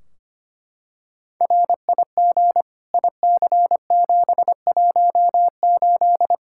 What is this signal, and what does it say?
Morse code, heard as a keyed tone, about 25 wpm, 700 Hz. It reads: RIG IC718
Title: Practice QSO - Exchange 2 - Info - 25wpm